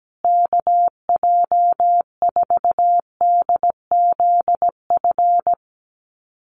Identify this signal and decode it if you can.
Morse code, heard as a keyed tone, about 17 wpm, 700 Hz. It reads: KJ4DZF
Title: USA Call Signs Repeated 2x 17wpm